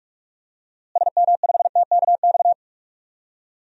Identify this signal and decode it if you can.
Morse code, heard as a keyed tone, about 43 wpm, 700 Hz. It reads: SM5TKX
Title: DXCC-100 Call Signs - 6 Characters - Repeated 2x - 45wpm